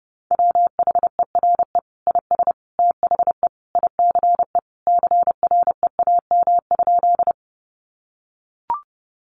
Morse code, heard as a keyed tone, about 30 wpm, 700 Hz. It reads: WHERE IS THE ICE CREAM?
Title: Sentences from Top 1500 Words 30wpm